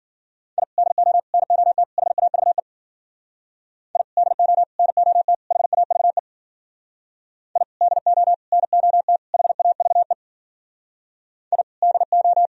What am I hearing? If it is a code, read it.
Morse code, heard as a keyed tone, about 50 wpm, 700 Hz. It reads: I DO NOT HAVE I DO NOT HAVE I DO NOT HAVE I DO
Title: Speed-Racing - Sets of 4 Words 40wpm